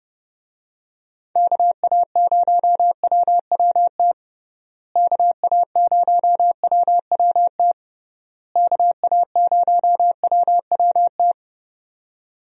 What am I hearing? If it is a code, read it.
Morse code, heard as a keyed tone, about 32 wpm, 700 Hz. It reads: KA0WWT KA0WWT KA0WWT
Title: USA Call Signs - 6 Characters - Repeated 3x - 30wpm